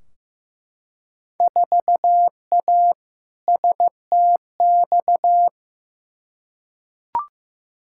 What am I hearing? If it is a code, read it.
Morse code, heard as a keyed tone, about 15 wpm, 700 Hz. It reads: 4A STX
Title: Field Day Exchanges - 15wpm